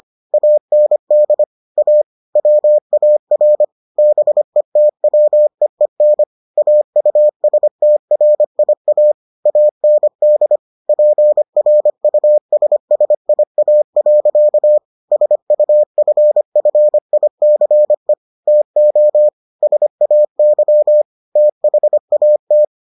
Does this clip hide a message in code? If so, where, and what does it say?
Yes: Morse code from 0.3 to 22.6 s, heard as a keyed tone, about 25 wpm, 600 Hz. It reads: AND A WAR BETWEEN AUSTRIA AND PRUSSIA. SUFFICE TO SAY THAT